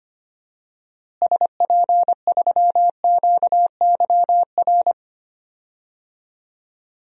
Morse code, heard as a keyed tone, about 25 wpm, 700 Hz. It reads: SP3QYR
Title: DXCC-100 Call Signs - 6 Characters - Repeated 3x - 25wpm